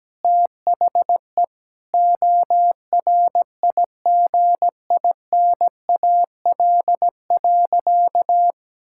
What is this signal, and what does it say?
Morse code, heard as a keyed tone, about 17 wpm, 700 Hz. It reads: THE ORIGINAL.
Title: Sentences from Top 1000 Words - Review Part 6 - 17wpm